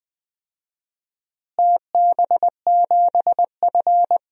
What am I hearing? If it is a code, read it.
Morse code, heard as a keyed tone, about 20 wpm, 700 Hz. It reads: TB7F
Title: DXCC-100 Call Signs - 4 Characters - Repeated 3x - 20wpm